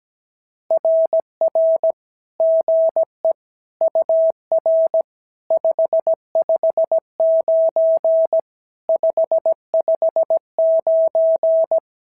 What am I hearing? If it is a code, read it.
Morse code, heard as a keyed tone, about 16 wpm, 650 Hz. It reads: RR GE UR 559 559
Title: SOTA QSO Practice 17wpm